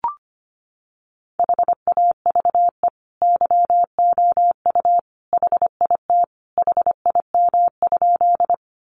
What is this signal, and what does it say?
Morse code, heard as a keyed tone, about 25 wpm, 700 Hz. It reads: HAVE YOU HIT HIM?
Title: Sentences from Top 1200 Words 25wpm